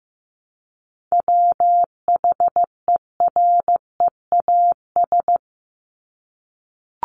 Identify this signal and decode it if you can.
Morse code, heard as a keyed tone, about 15 wpm, 700 Hz. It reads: WHEREAS
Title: Top 1501-1750 Words 15wpm